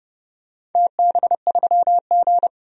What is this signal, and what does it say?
Morse code, heard as a keyed tone, about 29 wpm, 700 Hz. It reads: TB3G
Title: DXCC-100 Call Signs - 4 Characters - 30wpm